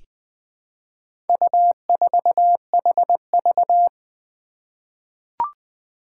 Morse code, heard as a keyed tone, about 20 wpm, 700 Hz. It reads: U4HV
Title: Four Challenging Letter-Number Combinations 20wpm